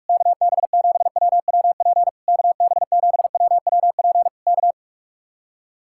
Morse code, heard as a keyed tone, about 49 wpm, 700 Hz. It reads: KB7WWP KB7WWP K